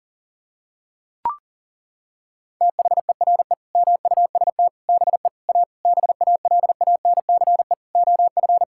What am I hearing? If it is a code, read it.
Morse code, heard as a keyed tone, about 40 wpm, 700 Hz. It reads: THERE MUST BE A BALANCE OF
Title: Sentences from Top 1000 Words - Review Part 6 - 40wpm